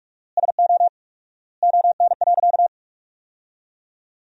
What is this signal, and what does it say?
Morse code, heard as a keyed tone, about 46 wpm, 700 Hz. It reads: SO ON.
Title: 4x Word-Spacing - Sentences from Top 100 Words - 45wpm